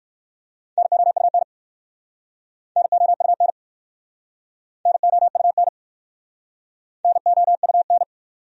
Morse code, heard as a keyed tone, about 59 wpm, 700 Hz. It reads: NOUN NOUN NOUN NOUN